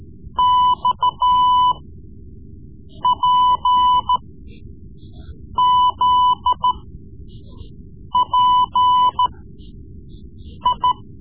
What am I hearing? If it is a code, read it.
Morse code, heard as a keyed tone, about 14 wpm, spaced as for 5 wpm, 1000 Hz. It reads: XPZPI
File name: Morse Lessons / Morse Lessons / Tape-3B